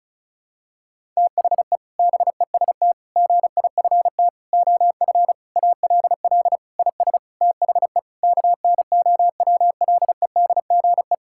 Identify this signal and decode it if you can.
Morse code, heard as a keyed tone, about 34 wpm, 700 Hz. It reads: THE BEST GIFT OF ALL IS THE KNOWLEDGE
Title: Sentences from Top 1000 Words - Review Part 6 - 35wpm